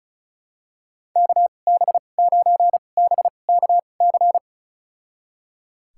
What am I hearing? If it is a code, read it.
Morse code, heard as a keyed tone, about 36 wpm, spaced as for 25 wpm, 700 Hz. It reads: KB9BKC
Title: USA Call Signs - 2x Character Spacing - 35wpm